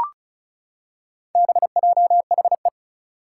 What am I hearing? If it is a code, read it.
Morse code, heard as a keyed tone, about 36 wpm, 700 Hz. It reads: BJHE
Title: Four Challenging Letter-Number Combinations 35wpm